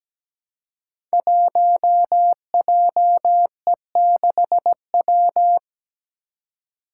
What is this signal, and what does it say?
Morse code, heard as a keyed tone, about 17 wpm, 700 Hz. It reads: 1JE6W
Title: Five Challenging Letter-Number Combinations 17wpm